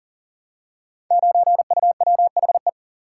Morse code, heard as a keyed tone, about 42 wpm, 700 Hz. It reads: 9UWHE
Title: Code Groups - 5 Letter:Numbers - 40wpm